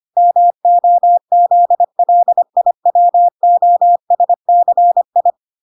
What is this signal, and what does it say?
Morse code, heard as a keyed tone, about 25 wpm, 700 Hz. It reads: MOZLIWOSCI